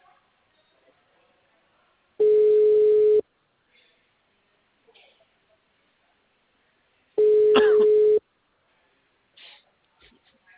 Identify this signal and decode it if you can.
Ringback tone (Europe, 425 Hz).
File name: out- -2016- - - .5013